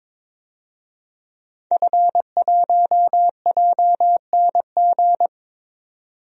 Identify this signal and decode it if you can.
Morse code, heard as a keyed tone, about 22 wpm, 700 Hz. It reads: F1JNG
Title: DXCC-100 Call Signs - 5 Characters - 22wpm